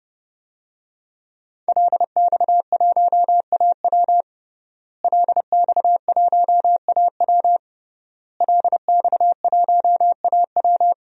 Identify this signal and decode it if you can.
Morse code, heard as a keyed tone, about 30 wpm, 700 Hz. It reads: LX1AW LX1AW LX1AW
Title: DXCC-100 Call Signs - 5 Characters - Repeated 3x - 30wpm